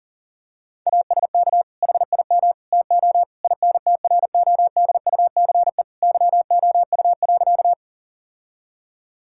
Morse code, heard as a keyed tone, about 40 wpm, 700 Hz. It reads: ASK HIM TO INTRODUCE YOU.
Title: Sentences from Top 1600 Words 40wpm